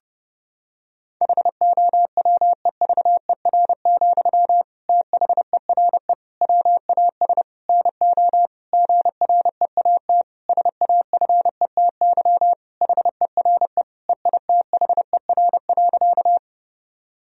Morse code, heard as a keyed tone, about 30 wpm, 700 Hz. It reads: HOWEVER, THERE WAS NO GREAT SAFETY HERE EITHER.
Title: Sentences from Top 1000 Words - Review Part 4 - 30wpm